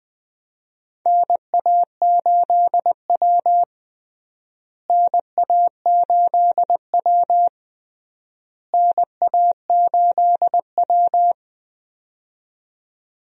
Morse code, heard as a keyed tone, about 19 wpm, 700 Hz. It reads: NA8W NA8W NA8W
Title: USA Call Signs - 4 Characters - Repeated 3x - 20wpm